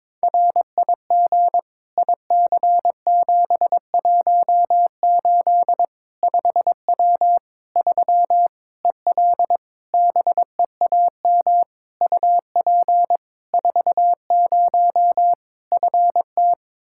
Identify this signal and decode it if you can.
Morse code, heard as a keyed tone, about 22 wpm, 700 Hz. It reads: RIG IC718 5W 3 EL BEAM UP 40 FT
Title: K1USN SST QSO Practice 22wpm